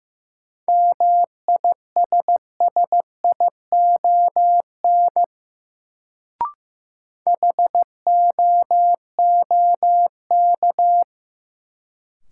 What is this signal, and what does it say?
Morse code, heard as a keyed tone, about 15 wpm, 700 Hz. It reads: MISSION HOOK
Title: Sentences from Top 1600 Words - New Word Drill - 15wpm